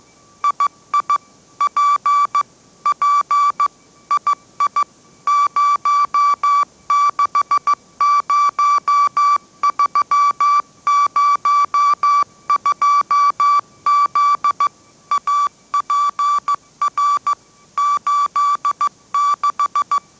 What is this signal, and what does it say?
Morse code, heard as a keyed tone, about 15 wpm, 1160 Hz. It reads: II P P II 060302Z APR 86
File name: mrx / mrx066 16wpm